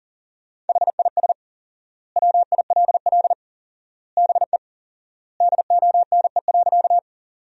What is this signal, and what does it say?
Morse code, heard as a keyed tone, about 40 wpm, 700 Hz. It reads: HIS WILL BE DONE.
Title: 4x Word-Spacing - Sentences from Top 500 Words - 40wpm